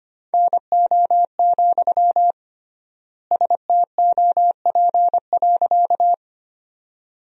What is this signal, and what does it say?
Morse code, heard as a keyed tone, about 25 wpm, 700 Hz. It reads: NO, STOP.
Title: 3x Word-Spacing - Sentences from Top 300 Words - 25wpm